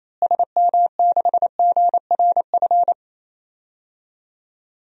Morse code, heard as a keyed tone, about 28 wpm, 700 Hz. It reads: SM6GRF
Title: DXCC-100 Call Signs - 6 Characters - Repeated 2x - 28wpm